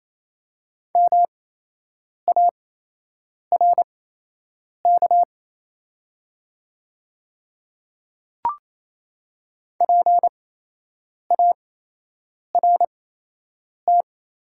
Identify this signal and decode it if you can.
Morse code, heard as a keyed tone, about 28 wpm, spaced as for 8 wpm, 700 Hz. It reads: MARK PART
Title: 8x Character Spacing - Top 300 Words - 28wpm